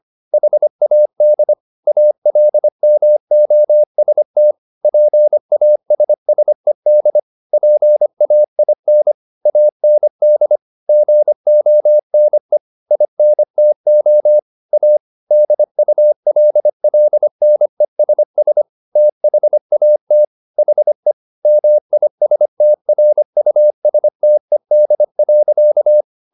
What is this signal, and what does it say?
Morse code, heard as a keyed tone, about 25 wpm, 600 Hz. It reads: HAD ALMOST PASSED PAIN AND GONE INTO A DULLNESS THAT HE MISTRUSTED.